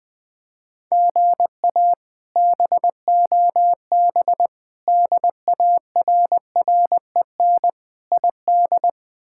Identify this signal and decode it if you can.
Morse code, heard as a keyed tone, about 21 wpm, 700 Hz. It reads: GA BOB DARREN ID
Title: K1USN SST QSO Practice - 20wpm